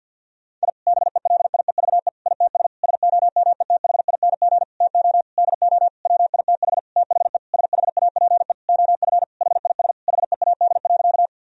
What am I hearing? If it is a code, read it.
Morse code, heard as a keyed tone, about 50 wpm, 700 Hz. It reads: I BELIEVE ITS SOMETHING TO DO WITH THE SHAPE OF HIS HEAD.
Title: Sentences from Top 1000 Words - Review Part 3 - 50wpm